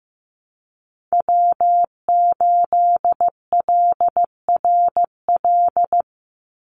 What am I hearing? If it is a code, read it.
Morse code, heard as a keyed tone, about 15 wpm, 700 Hz. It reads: W8LRL
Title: USA Call Signs 15wpm